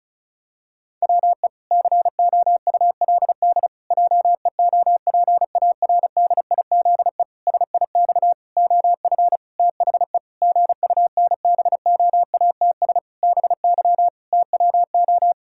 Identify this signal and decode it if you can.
Morse code, heard as a keyed tone, about 36 wpm, 700 Hz. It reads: WE COULD JEOPARDIZE SIX OF THE GUNBOATS BY TWO
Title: Pangram Challenge v2 - 35wpm